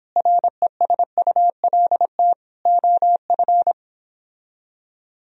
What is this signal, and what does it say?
Morse code, heard as a keyed tone, about 29 wpm, 700 Hz. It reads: RESULT OF